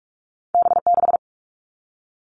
Morse code, heard as a keyed tone, about 46 wpm, 700 Hz. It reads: B6
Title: All Cut Numbers - 2-Digit 45wpm